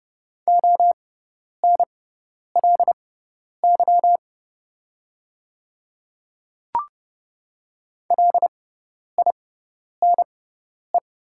Morse code, heard as a keyed tone, about 30 wpm, spaced as for 10 wpm, 700 Hz. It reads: ONLY LINE
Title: 6x Character Spacing - Top 300 Words - 30wpm